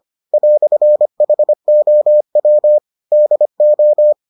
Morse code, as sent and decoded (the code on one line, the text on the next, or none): .-..-. .... --- .-- / -.. ---
"HOW DO